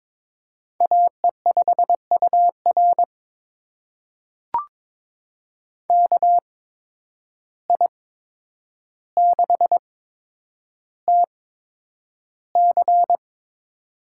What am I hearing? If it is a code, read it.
Morse code, heard as a keyed tone, about 23 wpm, spaced as for 9 wpm, 700 Hz. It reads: AE5UR KI6TC
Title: USA Call Signs - 5 Characters - 8x Character Spacing - 22wpm